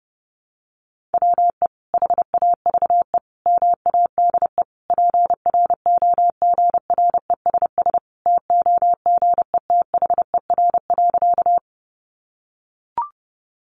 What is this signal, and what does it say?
Morse code, heard as a keyed tone, about 30 wpm, 700 Hz. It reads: WE HAVE MADE PROGRESS TOGETHER.
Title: Sentences from Top 1400 Words 30wpm